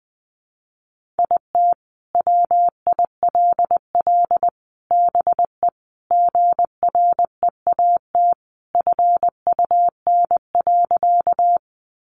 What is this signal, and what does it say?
Morse code, heard as a keyed tone, about 20 wpm, 700 Hz. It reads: IT WILL BE GREAT FUN.
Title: Sentences from Top 700 Words 20wpm